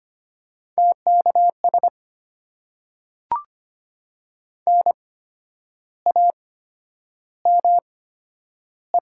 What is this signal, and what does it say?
Morse code, heard as a keyed tone, about 25 wpm, spaced as for 10 wpm, 700 Hz. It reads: TKS NAME
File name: Top 90 QSO Elements - 8x Character Spacing - 25wpm